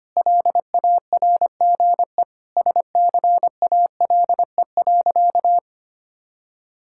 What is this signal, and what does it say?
Morse code, heard as a keyed tone, about 25 wpm, 700 Hz. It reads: LARGE SCALE.
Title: Sentences from Top 1000 Words - Review Part 5 - 25wpm